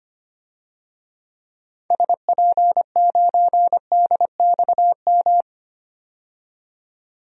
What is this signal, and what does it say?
Morse code, heard as a keyed tone, about 25 wpm, 700 Hz. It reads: SP9DXM
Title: DXCC-100 Call Signs - 6 Characters - 25wpm